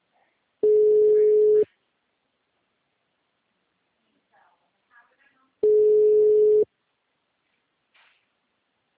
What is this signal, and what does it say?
Ringback tone (Europe, 425 Hz), with speech.